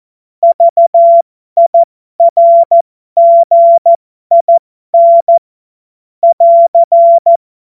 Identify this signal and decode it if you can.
Morse code, heard as a keyed tone, about 14 wpm, spaced as for 12 wpm, 680 Hz. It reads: VIRGIN <AR>
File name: Morsekurs / Witze